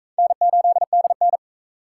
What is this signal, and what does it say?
Morse code, heard as a keyed tone, about 40 wpm, 700 Hz. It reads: N8DN